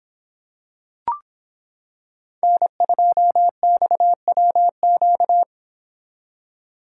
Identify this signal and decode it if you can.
Morse code, heard as a keyed tone, about 26 wpm, 700 Hz. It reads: N2XWQ